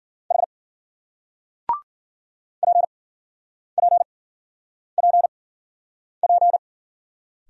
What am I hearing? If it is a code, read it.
Morse code, heard as a keyed tone, about 47 wpm, spaced as for 9 wpm, 700 Hz. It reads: H PPPP
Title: Speed-Racing Single Letters 40wpm